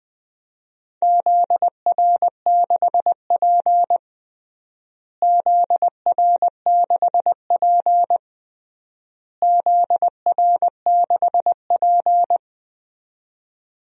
Morse code, heard as a keyed tone, about 20 wpm, 700 Hz. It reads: ZR6P ZR6P ZR6P
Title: DXCC-100 Call Signs - 4 Characters - Repeated 3x - 20wpm